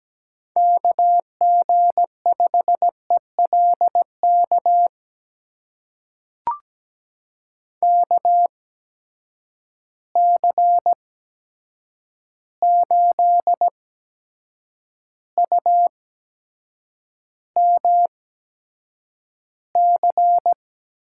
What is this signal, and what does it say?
Morse code, heard as a keyed tone, about 17 wpm, spaced as for 7 wpm, 700 Hz. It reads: KG5ELK KC8UMC
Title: USA Call Signs - 6 Characters - 8x Character Spacing - 17wpm